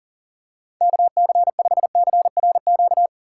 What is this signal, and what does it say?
Morse code, heard as a keyed tone, about 40 wpm, 700 Hz. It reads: KC5CRQ
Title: USA Call Signs - 6 Characters - 6x Character Spacing - 40wpm